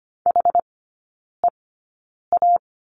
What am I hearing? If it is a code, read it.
Morse code, heard as a keyed tone, about 25 wpm, spaced as for 9 wpm, 690 Hz. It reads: HEA
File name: Lesson 15e - Number 5 - Three Characters 25wpm